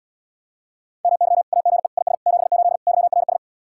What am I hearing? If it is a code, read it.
Morse code, heard as a keyed tone, about 80 wpm, 700 Hz. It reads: MY AGE IS 78 YRS